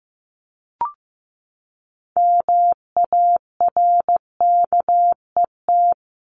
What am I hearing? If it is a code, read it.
Morse code, heard as a keyed tone, about 15 wpm, 700 Hz. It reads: MARKET
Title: ING Suffix - Top 100 Words with Base Words 15wpm